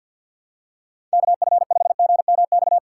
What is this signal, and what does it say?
Morse code, heard as a keyed tone, about 52 wpm, 700 Hz. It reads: KF5GMX
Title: USA Call Signs - 6 Characters - 6x Character Spacing - 50wpm